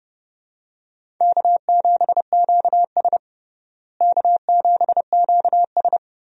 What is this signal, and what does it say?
Morse code, heard as a keyed tone, about 30 wpm, 700 Hz. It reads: K7QS K7QS
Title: USA Call Signs - 4 Characters - Repeated 3x - 30wpm